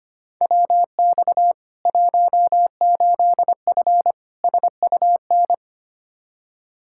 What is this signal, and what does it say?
Morse code, heard as a keyed tone, about 25 wpm, 700 Hz. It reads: WX 18F SUN
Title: Practice QSO - Exchange 2 - Info - 25wpm